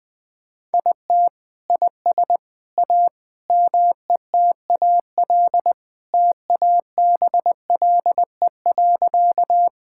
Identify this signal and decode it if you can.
Morse code, heard as a keyed tone, about 21 wpm, 700 Hz. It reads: IT IS A METAL TABLE.
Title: Sentences from Top 1000 Words - Review Part 6 - 20wpm